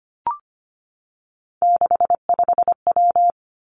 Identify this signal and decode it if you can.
Morse code, heard as a keyed tone, about 25 wpm, 700 Hz. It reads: 65W
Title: Three Challenging Letter-Number Combinations 25wpm